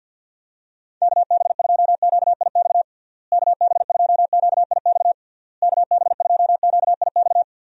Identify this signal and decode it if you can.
Morse code, heard as a keyed tone, about 52 wpm, 700 Hz. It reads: KB2QIX KB2QIX KB2QIX
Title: USA Call Signs - 6 Characters - Repeated 3x - 50wpm